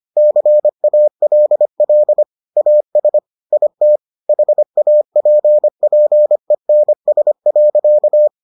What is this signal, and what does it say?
Morse code, heard as a keyed tone, about 25 wpm, 600 Hz. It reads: CALL AS IT HAPPENS.